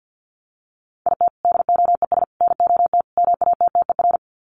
Morse code, heard as a keyed tone, about 50 wpm, 700 Hz. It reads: IT DOES NOT MATTER
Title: Sets of 4 Words 50wpm